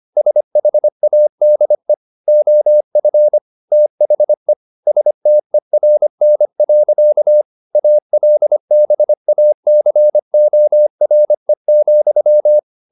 Morse code, heard as a keyed tone, about 25 wpm, 600 Hz. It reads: SHADE OF THE STERN. ALBACORE,
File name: the old man and the sea part 1 25wpm